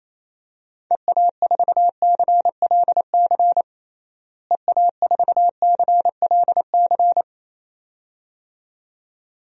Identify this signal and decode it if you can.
Morse code, heard as a keyed tone, about 27 wpm, 700 Hz. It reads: EA4CLC EA4CLC
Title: DXCC-100 Call Signs - 6 Characters - Repeated 2x - 28wpm